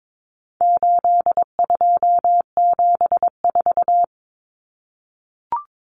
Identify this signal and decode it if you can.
Morse code, heard as a keyed tone, about 22 wpm, 700 Hz. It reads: 8274
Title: Numbers - 4-Digit 22wpm